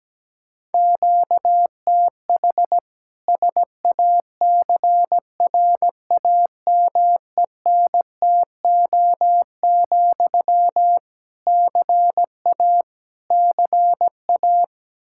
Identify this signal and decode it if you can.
Morse code, heard as a keyed tone, about 17 wpm, 700 Hz. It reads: QTH SACRAMENTO, CA CA
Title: Practice QSO - Exchange 1 - Beginning - 17wpm